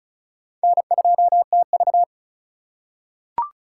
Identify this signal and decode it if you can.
Morse code, heard as a keyed tone, about 36 wpm, 700 Hz. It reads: N2TV
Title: USA Call Signs - 4 Characters - 3x Character Spacing - 35wpm